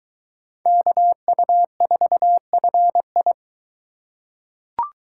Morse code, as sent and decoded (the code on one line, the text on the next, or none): -.- ..- ....- ..-. ..
KU4FI